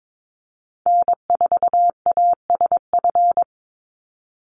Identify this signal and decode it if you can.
Morse code, heard as a keyed tone, about 22 wpm, 690 Hz. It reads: N4ASF
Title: Call Signs Repeated 3x - 22wpm